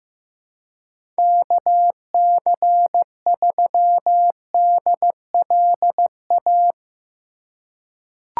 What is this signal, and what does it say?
Morse code, heard as a keyed tone, about 15 wpm, 700 Hz. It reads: KC3DLA
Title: USA Call Signs - 6 Characters - 3x Character Spacing - 15wpm